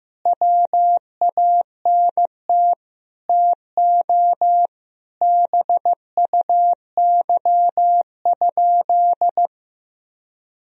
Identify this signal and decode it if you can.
Morse code, heard as a keyed tone, about 15 wpm, 700 Hz. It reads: WANT TO BUY?
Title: Sentences from Top 1000 Words - Review Part 4 - 15wpm